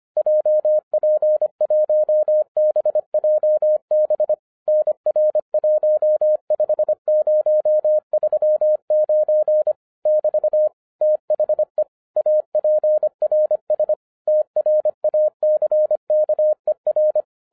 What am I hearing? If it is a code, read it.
Morse code, heard as a keyed tone, about 25 wpm, 610 Hz. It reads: JP1BJB NR15039 <BT> THE APRS TRACKER
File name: 2011-10-mp3 / n09-CW TO PS2 CONVERTER 25wpm